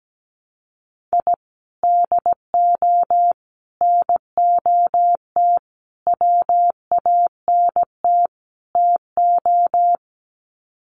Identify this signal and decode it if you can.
Morse code, heard as a keyed tone, about 17 wpm, 700 Hz. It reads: I DO NOT WANT TO
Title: Sets of 5 Words Encore2 17wpm